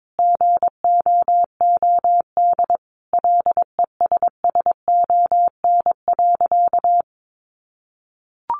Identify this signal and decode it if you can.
Morse code, heard as a keyed tone, about 22 wpm, 700 Hz. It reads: GOOD LESSON.
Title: Sentences from Top 1200 Words 22wpm